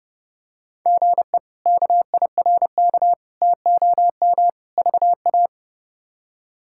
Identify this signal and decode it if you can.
Morse code, heard as a keyed tone, about 30 wpm, 700 Hz. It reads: GE KIRK TOM VA
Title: K1USN SST QSO Practice - 30wpm